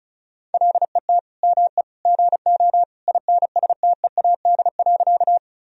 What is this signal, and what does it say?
Morse code, heard as a keyed tone, about 34 wpm, 700 Hz. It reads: LET ME GO INSTEAD.
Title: Sentences from Top 1000 Words - Review Part 4 - 35wpm